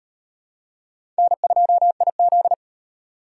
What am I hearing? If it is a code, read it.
Morse code, heard as a keyed tone, about 35 wpm, 700 Hz. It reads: N2IZ